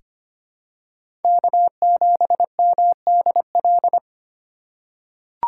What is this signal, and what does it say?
Morse code, heard as a keyed tone, about 25 wpm, 700 Hz. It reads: K7MDL
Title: USA Call Signs - 5 Characters - 4x Character Spacing - 25wpm